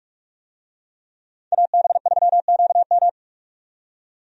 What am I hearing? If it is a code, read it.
Morse code, heard as a keyed tone, about 46 wpm, 700 Hz. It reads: AB3QM
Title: USA Call Signs - 5 Characters - 45wpm